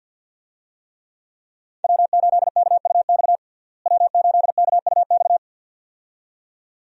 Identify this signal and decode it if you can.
Morse code, heard as a keyed tone, about 46 wpm, 700 Hz. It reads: W8KUX W8KUX
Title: USA Call Signs - 5 Characters - Repeated 2x - 50wpm